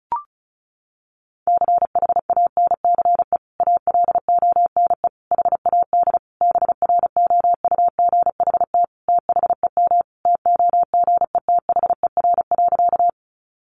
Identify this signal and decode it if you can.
Morse code, heard as a keyed tone, about 34 wpm, 700 Hz. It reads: CHANCE ALONE HAD BROUGHT THEM TOGETHER.
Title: Sentences from Top 1000 Words 35wpm